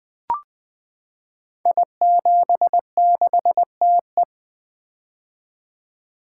Morse code, heard as a keyed tone, about 20 wpm, 700 Hz. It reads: I76TE
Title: Code Groups - 5 Letter:Numbers - 20wpm